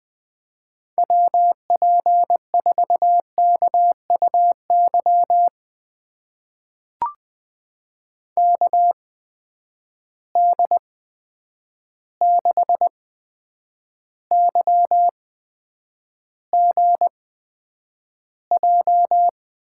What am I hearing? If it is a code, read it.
Morse code, heard as a keyed tone, about 21 wpm, spaced as for 9 wpm, 700 Hz. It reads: WP4KUY KD6YGJ
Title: USA Call Signs - 8x Character Spacing - 20wpm